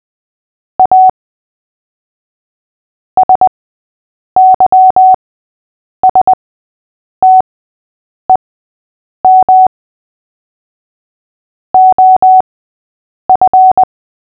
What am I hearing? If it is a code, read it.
Morse code, heard as a keyed tone, about 20 wpm, spaced as for 8 wpm, 750 Hz. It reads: A SYSTEM OF
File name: cw / fists / Short-Story-6-8wpm